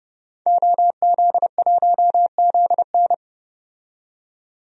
Morse code, heard as a keyed tone, about 30 wpm, 690 Hz. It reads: OZ1ZN
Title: DXCC-100 Call Signs - 5 Characters - 30wpm